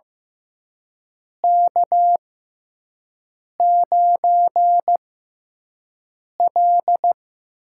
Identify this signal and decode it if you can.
Morse code, heard as a keyed tone, about 15 wpm, spaced as for 5 wpm, 700 Hz. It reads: K9L
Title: USA Call Signs - 4 Characters - 6x Character Spacing - 15wpm